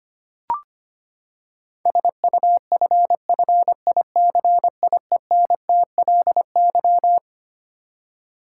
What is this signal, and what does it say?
Morse code, heard as a keyed tone, about 25 wpm, 700 Hz. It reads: SUFFICIENTLY
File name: Twelve Letter Words - 25wpm